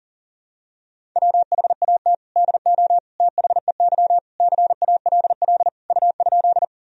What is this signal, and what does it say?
Morse code, heard as a keyed tone, about 40 wpm, 700 Hz. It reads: WHAT DO THEY CALL U?
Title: Sentences from Top 100 Words and Abbreviations 40wpm